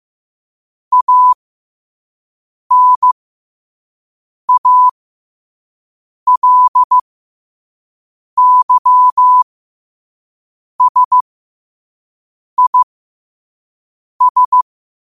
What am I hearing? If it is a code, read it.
Morse code, heard as a keyed tone, about 15 wpm, spaced as for 5 wpm, 1000 Hz. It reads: ANALYSIS
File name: qotdfiles / 5wpm